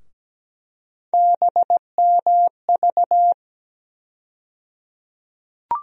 Morse code, heard as a keyed tone, about 17 wpm, 700 Hz. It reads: BMV
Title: USA Call Sign Postfixes - 3 Letters 17wpm